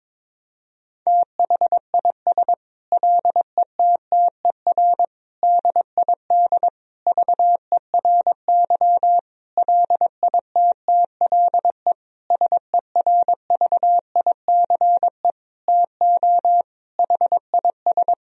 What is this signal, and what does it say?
Morse code, heard as a keyed tone, about 22 wpm, 700 Hz. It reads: THIS LETTER DID VERY LITTLE SERVICE TO HIS
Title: Sentences from Top 1000 Words - Review Part 4 - 22wpm